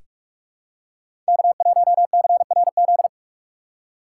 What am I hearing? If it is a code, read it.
Morse code, heard as a keyed tone, about 46 wpm, 700 Hz. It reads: K1CRZ
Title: USA Call Signs - 6x Character Spacing - 45wpm